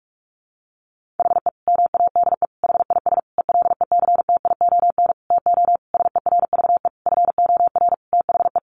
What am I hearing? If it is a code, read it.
Morse code, heard as a keyed tone, about 44 wpm, 700 Hz. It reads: HE MADE HIS ELECTION TO SERVE FOR THE
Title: Sentences from Top 1500 Words 45wpm